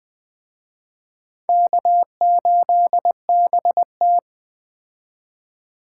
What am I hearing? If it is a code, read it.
Morse code, heard as a keyed tone, about 20 wpm, 700 Hz. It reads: K8BT